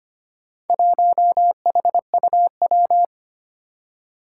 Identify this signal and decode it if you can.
Morse code, heard as a keyed tone, about 25 wpm, 700 Hz. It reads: 1HUW
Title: Four Challenging Letter-Number Combinations 25wpm